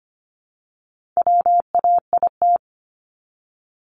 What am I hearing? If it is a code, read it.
Morse code, heard as a keyed tone, about 25 wpm, 700 Hz. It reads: WAIT
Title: ING Suffix - Top 100 Words with Base Words 25wpm